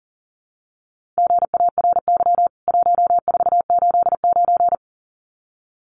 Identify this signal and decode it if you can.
Morse code, heard as a keyed tone, about 40 wpm, 700 Hz. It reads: GARY 1489
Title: CWT Exchange 40wpm